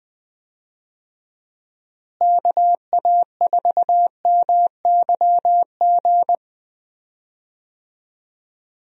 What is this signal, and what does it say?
Morse code, heard as a keyed tone, about 20 wpm, 700 Hz. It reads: KA4MYG
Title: USA Call Signs - 6 Characters - Repeated 3x - 20wpm